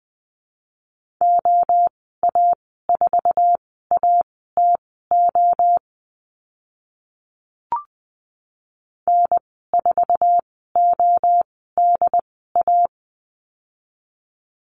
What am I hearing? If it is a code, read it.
Morse code, heard as a keyed tone, about 20 wpm, spaced as for 15 wpm, 700 Hz. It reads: OA4ATO N4ODA
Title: 076 Lesson 13g - Letter L - Call Signs 20wpm